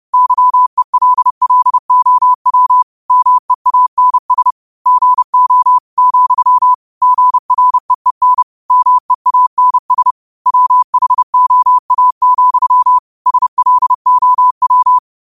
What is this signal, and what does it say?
Morse code, heard as a keyed tone, about 30 wpm, 1000 Hz. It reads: YELLOW MEANS GO , GREEN MEANS WHOA, SLOW
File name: qotdfiles / 30wpm